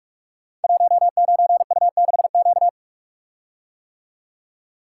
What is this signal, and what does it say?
Morse code, heard as a keyed tone, about 44 wpm, 700 Hz. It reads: 19U6Q
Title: Code Groups - 5 Letter:Numbers - 45wpm